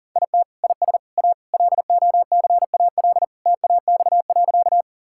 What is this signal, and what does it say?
Morse code, heard as a keyed tone, about 40 wpm, 700 Hz. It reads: IT IS A LOCAL TAX.
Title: Sentences from Top 1000 Words - Review Part 5 - 40wpm